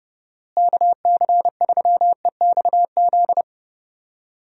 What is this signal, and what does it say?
Morse code, heard as a keyed tone, about 32 wpm, 700 Hz. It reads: KC3EXZ
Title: USA Call Signs - 6 Characters - Repeated 3x - 30wpm